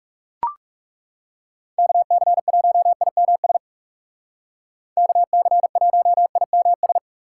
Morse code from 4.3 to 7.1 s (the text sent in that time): KC1IMS